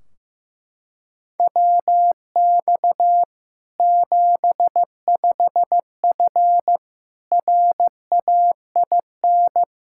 Morse code, heard as a keyed tone, about 15 wpm, 700 Hz. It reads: WX 75F RAIN
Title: Practice QSO - Exchange 2 - Info - 15wpm